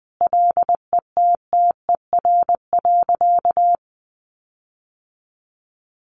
Morse code, heard as a keyed tone, about 20 wpm, 700 Hz. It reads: LETTER.
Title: Sentences from Top 1500 Words 20wpm